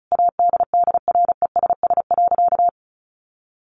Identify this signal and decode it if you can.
Morse code, heard as a keyed tone, about 34 wpm, 700 Hz. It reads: ADDRESS.
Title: Sentences from Top 1000 Words 35wpm